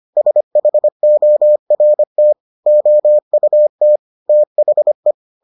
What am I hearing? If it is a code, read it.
Morse code, heard as a keyed tone, about 25 wpm, 600 Hz. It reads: SHORT OUT THE